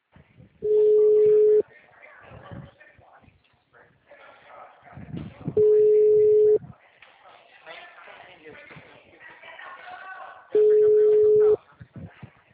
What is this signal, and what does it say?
Ringback tone (Europe, 425 Hz), with speech.